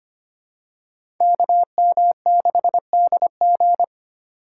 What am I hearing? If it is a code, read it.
Morse code, heard as a keyed tone, about 25 wpm, 700 Hz. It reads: KM6DG
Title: USA Call Signs - 4x Character Spacing - 25wpm